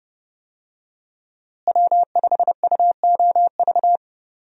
Morse code, heard as a keyed tone, about 30 wpm, 700 Hz. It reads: W5UOV
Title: USA Call Signs - 5 Characters - Repeated 2x - 30wpm